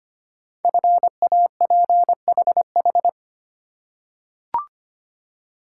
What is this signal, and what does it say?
Morse code, heard as a keyed tone, about 25 wpm, 700 Hz. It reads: FAPHH
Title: Code Groups - 5 Letters - 25wpm